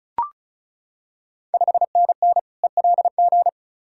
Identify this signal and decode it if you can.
Morse code, heard as a keyed tone, about 36 wpm, 690 Hz. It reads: 5NN ELG
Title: Ontario QSO Party Exchange 35wpm